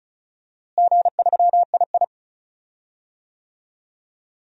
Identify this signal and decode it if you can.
Morse code, heard as a keyed tone, about 36 wpm, 700 Hz. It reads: G3II
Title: DXCC-100 Call Signs - 4 Characters - Repeated 3x - 35wpm